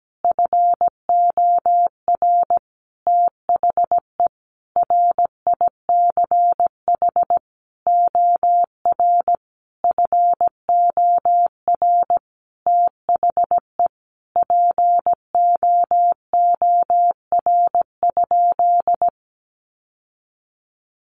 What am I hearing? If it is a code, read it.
Morse code, heard as a keyed tone, about 17 wpm, 700 Hz. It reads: FOR THE RICH OR FOR THE POOR?
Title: Sentences from Top 1100 Words 17wpm